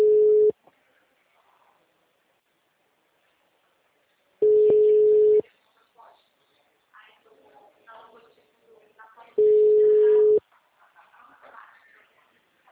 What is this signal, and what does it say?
Ringback tone (Europe, 425 Hz), with speech.